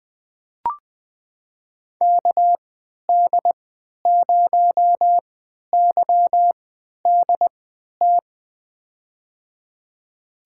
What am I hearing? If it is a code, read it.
Morse code, heard as a keyed tone, about 21 wpm, spaced as for 12 wpm, 700 Hz. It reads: KD0YDT
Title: USA Call Signs - 6 Characters - 3x Character Spacing - 20wpm